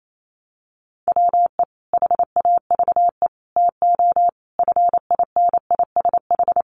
Morse code, heard as a keyed tone, about 28 wpm, 700 Hz. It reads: WE HAVE TO FINISH
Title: Sentences from Top 1200 Words 28wpm